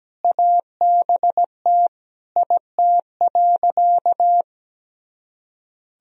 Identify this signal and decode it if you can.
Morse code, heard as a keyed tone, about 17 wpm, 700 Hz. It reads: ABT IT.
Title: Sentences from Top 500 Words and Abbreviations 17wpm